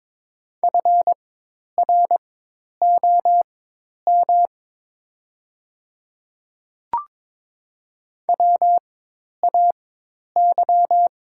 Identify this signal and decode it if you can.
Morse code, heard as a keyed tone, about 22 wpm, spaced as for 10 wpm, 700 Hz. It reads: FROM WAY
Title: 4x Character Spacing - Top 100 Words - 22wpm